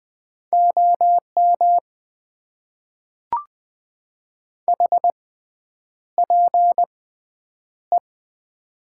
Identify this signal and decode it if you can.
Morse code, heard as a keyed tone, about 21 wpm, spaced as for 9 wpm, 700 Hz. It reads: OM HPE
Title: Top 90 QSO Elements - 6x Character Spacing - 20wpm